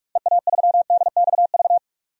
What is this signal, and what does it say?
Morse code, heard as a keyed tone, about 43 wpm, 700 Hz. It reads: EA3DXV
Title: DXCC-100 Call Signs - 6 Characters - 45wpm